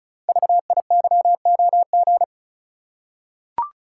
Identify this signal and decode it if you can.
Morse code, heard as a keyed tone, about 36 wpm, 700 Hz. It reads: VIYOG
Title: Code Groups - 5 Letters - 35wpm